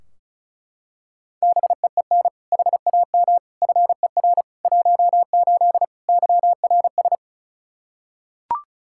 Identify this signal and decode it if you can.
Morse code, heard as a keyed tone, about 34 wpm, 700 Hz. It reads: BEEN HAM FER 18 YRS
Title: Practice QSO - Exchange 3 - More Info - 35wpm